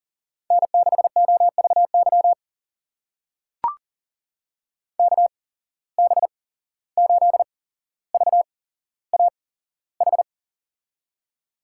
Morse code, heard as a keyed tone, about 40 wpm, spaced as for 16 wpm, 700 Hz. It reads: N6OVY KB8VAH
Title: USA Call Signs - 8x Character Spacing - 40wpm